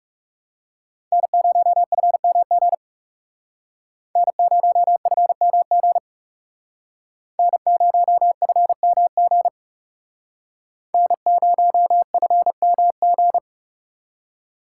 Morse code, heard as a keyed tone, about 36 wpm, 700 Hz. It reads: N0FMG N0FMG N0FMG N0FMG
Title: Speed Racing - USA 1x3 Call Signs 30wpm